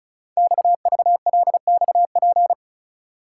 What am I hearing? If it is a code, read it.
Morse code, heard as a keyed tone, about 36 wpm, 690 Hz. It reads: XVLXP
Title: Code Groups - 5 Letters - 35wpm